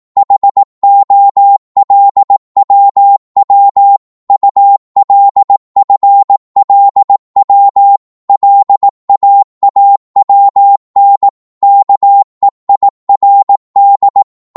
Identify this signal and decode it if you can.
Morse code, heard as a keyed tone, about 18 wpm, 800 Hz. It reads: HOLWW ULFLW LAAWN KEIRD
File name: Morse / ltr-18wpm